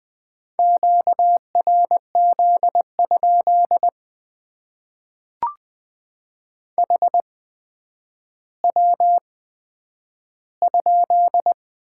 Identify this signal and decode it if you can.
Morse code, heard as a keyed tone, about 20 wpm, spaced as for 10 wpm, 700 Hz. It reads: QRZ? HW?
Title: Top 90 QSO Elements - 8x Character Spacing - 20wpm